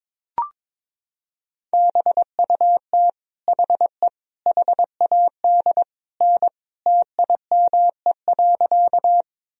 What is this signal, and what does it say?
Morse code, heard as a keyed tone, about 22 wpm, 700 Hz. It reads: BUT HE HAD N TIME.
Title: Sentences from Top 100 Words and Abbreviations 22wpm